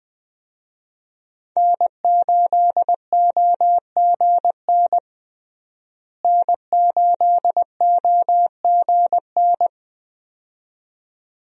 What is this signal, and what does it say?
Morse code, heard as a keyed tone, about 20 wpm, 700 Hz. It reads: N8OGN N8OGN
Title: USA Call Signs - 5 Characters - Repeated 2x - 20wpm